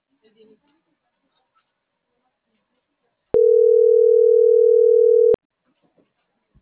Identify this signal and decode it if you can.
Ringback tone (North America), 440 + 480 Hz.